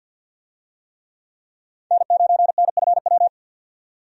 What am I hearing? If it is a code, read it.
Morse code, heard as a keyed tone, about 52 wpm, 700 Hz. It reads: N9NFW
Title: USA Call Signs - 5 Characters - 50wpm